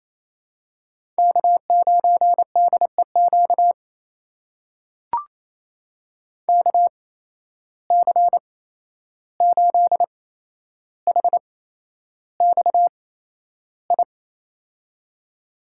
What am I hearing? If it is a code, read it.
Morse code, heard as a keyed tone, about 27 wpm, spaced as for 11 wpm, 700 Hz. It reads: K9DEQ KC8HXI
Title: USA Call Signs - 8x Character Spacing - 28wpm